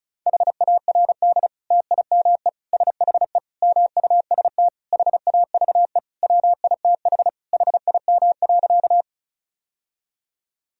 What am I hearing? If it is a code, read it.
Morse code, heard as a keyed tone, about 36 wpm, 700 Hz. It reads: HARD TIME SHE MUST HAVE WITH HIM.
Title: Sentences from Top 1000 Words - Review Part 1 - 35wpm